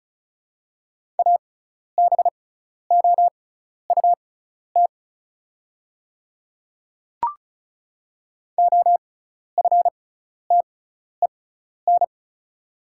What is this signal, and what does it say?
Morse code, heard as a keyed tone, about 34 wpm, spaced as for 12 wpm, 700 Hz. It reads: ABOUT OFTEN
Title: 6x Character Spacing - Top 300 Words - 35wpm